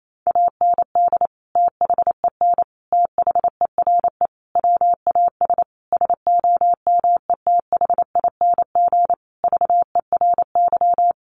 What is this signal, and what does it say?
Morse code, heard as a keyed tone, about 28 wpm, 700 Hz. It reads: AND THEN THERE WAS SOMETHING VERY
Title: Sentences from Top 1400 Words 28wpm